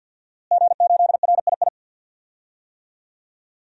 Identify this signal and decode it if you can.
Morse code, heard as a keyed tone, about 47 wpm, 700 Hz. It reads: G8RII
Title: DXCC-100 Call Signs - 5 Characters - Repeated 3x - 50wpm